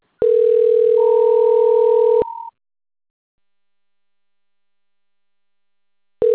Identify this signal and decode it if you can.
Ringback tone (North America), 440 + 480 Hz.